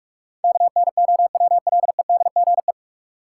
Morse code, heard as a keyed tone, about 41 wpm, 700 Hz. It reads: KNOWLEDGE
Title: Sentences from Top 600 Words - New Word Drill - 45wpm